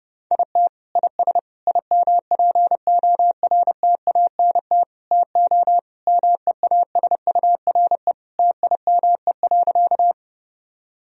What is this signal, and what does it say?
Morse code, heard as a keyed tone, about 32 wpm, 700 Hz. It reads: IT IS IMPORTANT TO MEASURE TIME.
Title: Sentences from Top 1000 Words - Review Part 2 - 30wpm